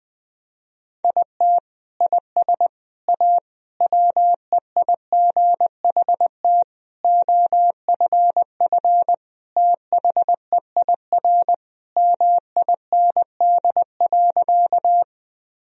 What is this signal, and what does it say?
Morse code, heard as a keyed tone, about 20 wpm, 700 Hz. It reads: IT IS A WEIGHT OFF THEIR MIND.
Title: Sentences from Top 1000 Words - Review Part 3 - 20wpm